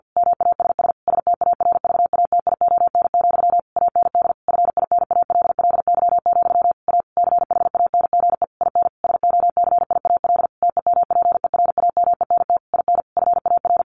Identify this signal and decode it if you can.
Morse code, heard as a keyed tone, about 50 wpm, 700 Hz. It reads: MASS STARVATION, AND FINALLY, A CHANGE IN SOCIAL TEMPERAMENT IN FAR
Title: A Sound Of Thunder 50wpm